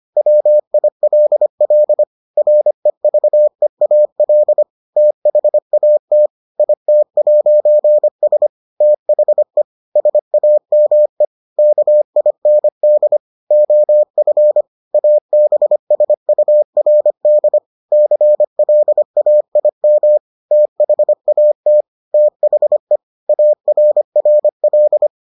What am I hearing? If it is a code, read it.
Morse code, heard as a keyed tone, about 25 wpm, 600 Hz. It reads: WILL REVEAL THAT IT'S THE SAME KIND OF ABSURD CLAIM THAT THE ARRL